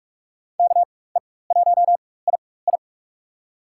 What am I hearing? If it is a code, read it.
Morse code, heard as a keyed tone, about 47 wpm, spaced as for 21 wpm, 700 Hz. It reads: KE1II
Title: USA Call Signs - 4x Character Spacing - 45wpm